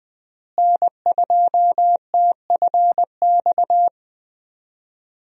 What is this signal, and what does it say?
Morse code, heard as a keyed tone, about 20 wpm, 700 Hz. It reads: N2TFX
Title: USA Call Signs - 5 Characters - Repeated 2x - 20wpm